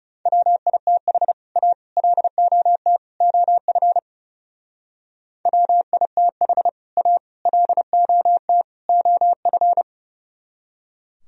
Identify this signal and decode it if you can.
Morse code, heard as a keyed tone, about 38 wpm, 700 Hz. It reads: WITH A LOT OF WITH A LOT OF